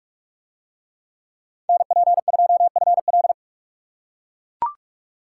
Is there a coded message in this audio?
Yes: Morse code from 1.7 to 3.3 s, heard as a keyed tone, about 40 wpm, 700 Hz. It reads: NP2FL